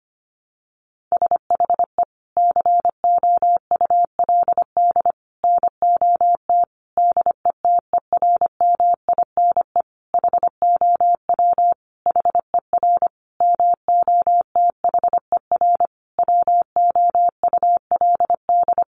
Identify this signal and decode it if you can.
Morse code, heard as a keyed tone, about 25 wpm, 700 Hz. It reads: SHE COULD NOT DETERMINE HOW HER MOTHER WOULD
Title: Sentences from Top 700 Words 25wpm